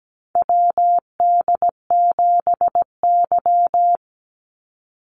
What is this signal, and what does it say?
Morse code, heard as a keyed tone, about 17 wpm, 700 Hz. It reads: WD7Y
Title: NA SOTA Chaser Callsigns 17wpm